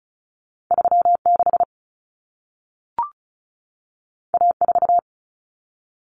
Morse code, heard as a keyed tone, about 36 wpm, 700 Hz. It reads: 36 A4
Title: Common Cut Numbers - 2-Digit 35wpm